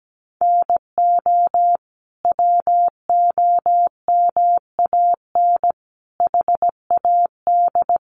Morse code, heard as a keyed tone, about 17 wpm, 700 Hz. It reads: NO WOMAN HAD
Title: Sentences from Top 1100 Words 17wpm